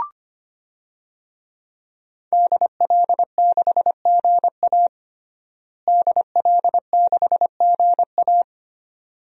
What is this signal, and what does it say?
Morse code, heard as a keyed tone, about 25 wpm, 700 Hz. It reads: DL6GA DL6GA
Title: DXCC-100 Call Signs - 5 Characters - Repeated 2x - 25wpm